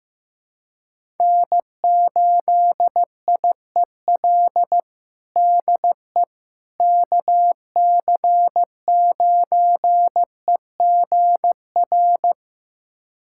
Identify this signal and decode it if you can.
Morse code, heard as a keyed tone, about 15 wpm, 700 Hz. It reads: N8IEL DE KC9EGR
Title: USA Callsign DE Callsign 15wpm